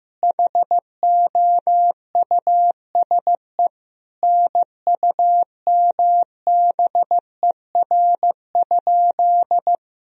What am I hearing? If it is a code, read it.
Morse code, heard as a keyed tone, about 15 wpm, 700 Hz. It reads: HOUSE NUMBER?
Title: Sentences from Top 1000 Words - Review Part 1 - 15wpm